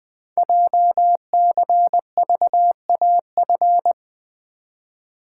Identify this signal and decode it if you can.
Morse code, heard as a keyed tone, about 20 wpm, 700 Hz. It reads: JCVAF
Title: Code Groups - 5 Letters - 20wpm